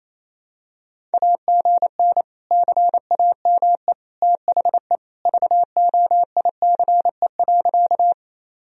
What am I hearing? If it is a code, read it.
Morse code, heard as a keyed tone, about 28 wpm, 700 Hz. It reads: AGN CAME THE VOICE.
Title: Sentences from Top 500 Words and Abbreviations 28wpm